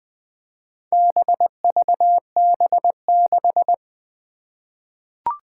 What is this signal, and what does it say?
Morse code, heard as a keyed tone, about 20 wpm, 700 Hz. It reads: BVB6
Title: Four Challenging Letter-Number Combinations 20wpm